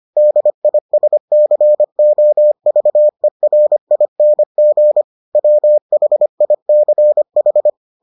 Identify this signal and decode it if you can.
Morse code, heard as a keyed tone, about 25 wpm, 600 Hz. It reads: DISCOVERING WHICH